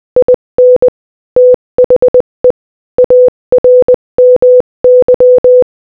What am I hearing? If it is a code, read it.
Morse code, heard as a keyed tone, about 20 wpm, 500 Hz. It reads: IN THE ARMY
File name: Morse-Code-Audio-Speed-20-4